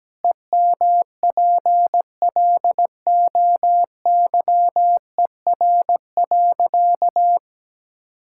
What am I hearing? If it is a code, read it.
Morse code, heard as a keyed tone, about 17 wpm, 700 Hz. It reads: EMPLOYER.
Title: Sentences from Top 1600 Words 17wpm